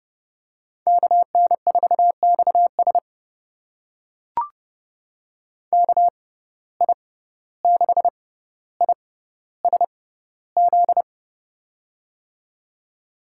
Morse code, heard as a keyed tone, about 30 wpm, spaced as for 15 wpm, 700 Hz. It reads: KN4XS KI6ISZ
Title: USA Call Signs - 6x Character Spacing - 30wpm